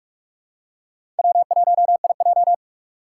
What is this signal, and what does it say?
Morse code, heard as a keyed tone, about 42 wpm, 700 Hz. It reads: W1IJ
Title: USA Call Signs - 4 Characters - 6x Character Spacing - 45wpm